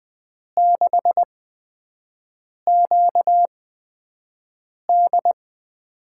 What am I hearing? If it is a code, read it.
Morse code, heard as a keyed tone, about 20 wpm, spaced as for 5 wpm, 700 Hz. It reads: 6QD
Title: USA Call Signs - 6 Characters - 8x Character Spacing - 20wpm